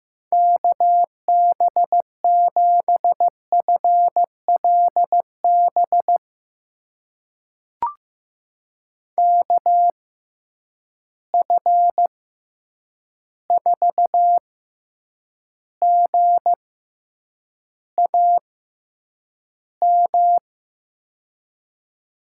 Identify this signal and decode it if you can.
Morse code, heard as a keyed tone, about 15 wpm, spaced as for 8 wpm, 700 Hz. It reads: KB7FLB KF4GAM
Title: USA Call Signs - 6 Characters - 6x Character Spacing - 15wpm